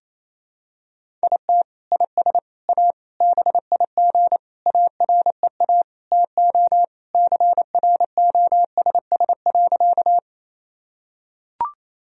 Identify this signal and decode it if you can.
Morse code, heard as a keyed tone, about 28 wpm, 690 Hz. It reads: IT IS A BIG AREA TO CROSS.
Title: Sentences from Top 1000 Words - Review Part 2 - 28wpm